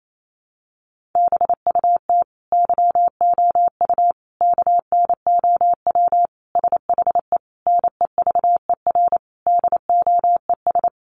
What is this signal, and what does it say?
Morse code, heard as a keyed tone, about 28 wpm, 700 Hz. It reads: BUT YOU KNOW SHE NEVER DOES
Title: Sentences from Top 1400 Words 28wpm